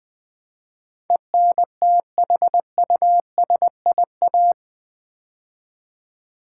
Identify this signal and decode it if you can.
Morse code, heard as a keyed tone, about 20 wpm, 700 Hz. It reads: ENTHUSIA
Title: Build A Word - 4 Letters At A Time - 10-Letter Words - 20wpm